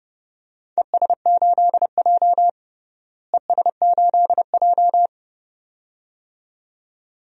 Morse code, heard as a keyed tone, about 30 wpm, 700 Hz. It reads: ES8J ES8J
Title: DXCC-100 Call Signs - 4 Characters - Repeated 2x - 30wpm